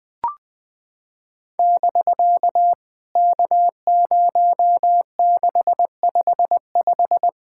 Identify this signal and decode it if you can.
Morse code, heard as a keyed tone, about 20 wpm, 700 Hz. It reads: <BK> K0655
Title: POTA QSO Practice - 20wpm